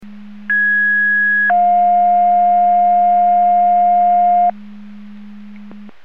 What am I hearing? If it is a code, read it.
Two-tone page, heard as a one-second tone, then a three-second tone, about 1650 then 710 Hz. It is heard from 0.5 to 4.5 s.
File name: Portage 154.385 Paging